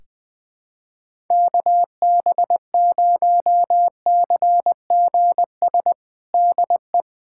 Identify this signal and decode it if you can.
Morse code, heard as a keyed tone, about 21 wpm, 700 Hz. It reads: KB0CGS DE
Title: USA Callsign DE Callsign - 6 Characters - 20wpm